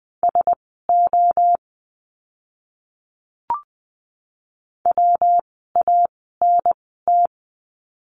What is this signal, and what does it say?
Morse code, heard as a keyed tone, about 20 wpm, spaced as for 15 wpm, 700 Hz. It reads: SO WANT
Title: 146 Lesson 24f - Letter y - Words 20wpm